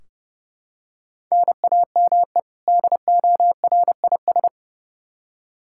Morse code, heard as a keyed tone, about 30 wpm, 700 Hz. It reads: NAME DORIS
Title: Practice QSO - Exchange 1 - Beginning - 30wpm